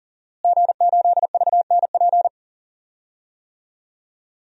Morse code, heard as a keyed tone, about 40 wpm, 700 Hz. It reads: G8VNP
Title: DXCC-100 Call Signs - 5 Characters - Repeated 3x - 40wpm